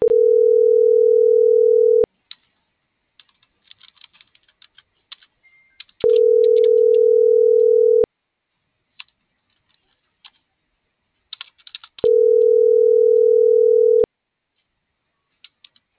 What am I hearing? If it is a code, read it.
Ringback tone (North America), 440 + 480 Hz.